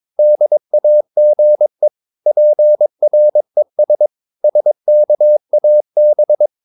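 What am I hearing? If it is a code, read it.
Morse code, heard as a keyed tone, about 22 wpm, 600 Hz. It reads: DAGE PRES SKAB
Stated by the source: Morse practice of short Danish words at 22 wpm, from CW med Gnister.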